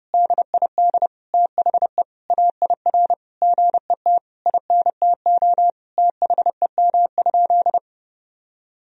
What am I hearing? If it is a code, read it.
Morse code, heard as a keyed tone, about 30 wpm, 700 Hz. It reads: DID THE AIR GET INTO THEM?
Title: Sentences from Top 1000 Words - Review Part 1 - 30wpm